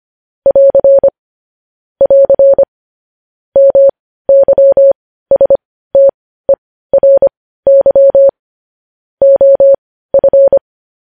Morse code, heard as a keyed tone, about 25 wpm, spaced as for 15 wpm, 550 Hz. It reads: <AR> <AR> MYSTERY OF
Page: News Headlines in Morse Code at 15 WPM